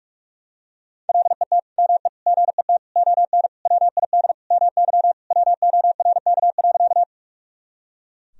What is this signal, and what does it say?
Morse code, heard as a keyed tone, about 41 wpm, 700 Hz. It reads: LET ME GET ON WID MY WORK.
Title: Sentences from Top 300 Words and Abbreviations 45wpm